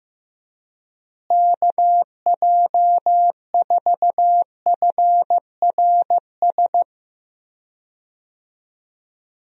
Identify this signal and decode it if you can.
Morse code, heard as a keyed tone, about 15 wpm, 700 Hz. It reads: KJ4FRS
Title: USA Call Signs - 6 Characters - Repeated 3x - 15wpm